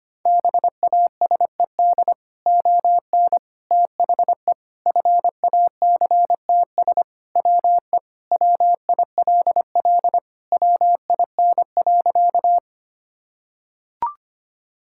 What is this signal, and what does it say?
Morse code, heard as a keyed tone, about 25 wpm, 700 Hz. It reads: BASED ON THE FACTS WE WILL WIN.
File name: practice / 25wpm / Sentences with 3-Word Phrases - 25wpm